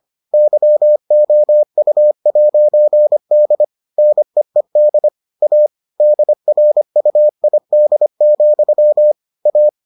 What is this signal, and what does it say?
Morse code, heard as a keyed tone, about 25 wpm, 600 Hz. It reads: YOU'D NEED A DRUID, A